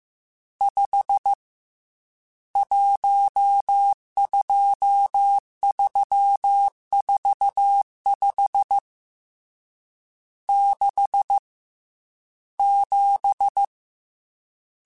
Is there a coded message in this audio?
Yes: Morse code from 0.6 to 13.6 s, heard as a keyed tone, about 15 wpm, 800 Hz. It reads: 5 12345 6 7